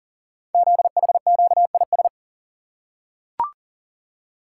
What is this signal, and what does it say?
Morse code, heard as a keyed tone, about 39 wpm, 700 Hz. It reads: ZHQIS
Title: Code Groups - 5 Letters - 40wpm